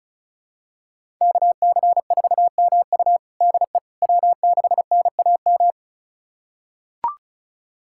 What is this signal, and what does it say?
Morse code, heard as a keyed tone, about 36 wpm, 700 Hz. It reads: KC4MU DE W6NAM
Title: USA Callsign DE Callsign - 5 Characters - 35wpm